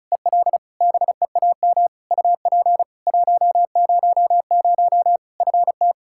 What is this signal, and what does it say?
Morse code, heard as a keyed tone, about 36 wpm, 700 Hz. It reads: EL BEAM UP 100 FT
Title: K1USN SST QSO Practice 35wpm